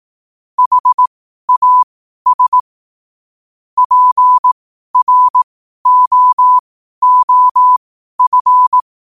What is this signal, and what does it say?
Morse code, heard as a keyed tone, about 18 wpm, spaced as for 12 wpm, 1000 Hz. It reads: HAS PROOF
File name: qotdfiles / 12wpm